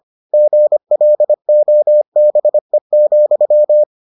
Morse code, heard as a keyed tone, about 25 wpm, 600 Hz. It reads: GLOBE,